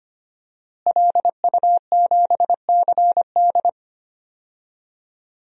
Morse code, heard as a keyed tone, about 25 wpm, 700 Hz. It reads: LU7CD
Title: DXCC-100 Call Signs - 5 Characters - Repeated 3x - 25wpm